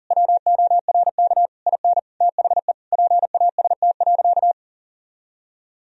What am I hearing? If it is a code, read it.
Morse code, heard as a keyed tone, about 37 wpm, 700 Hz. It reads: WORK IN THE PAST.
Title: Sentences from Top 1000 Words - Review Part 4 - 40wpm